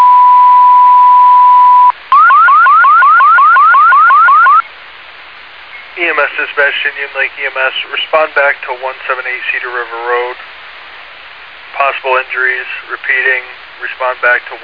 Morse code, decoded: none

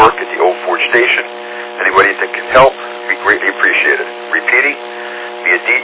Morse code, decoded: none

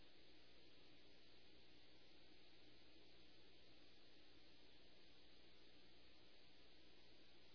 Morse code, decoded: none

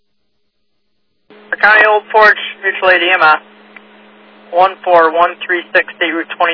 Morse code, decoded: none

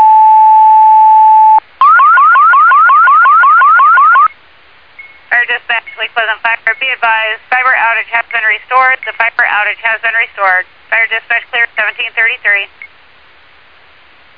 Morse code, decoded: none

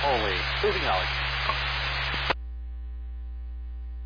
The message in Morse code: none

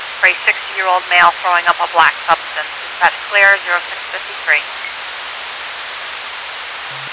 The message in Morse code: none

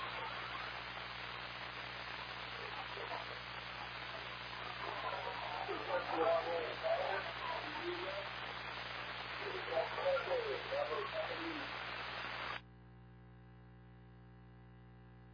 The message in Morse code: none

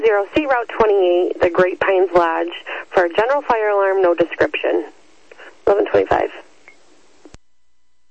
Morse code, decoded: none

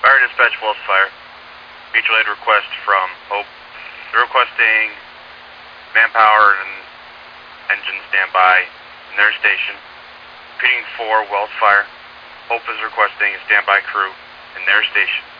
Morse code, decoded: none

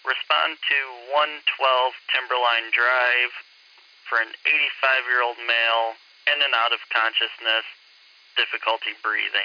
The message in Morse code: none

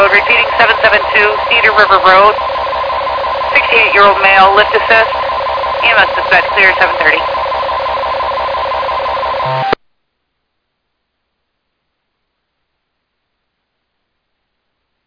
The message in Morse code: none

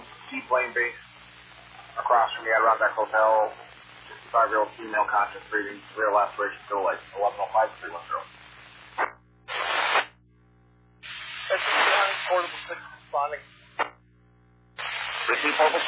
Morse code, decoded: none

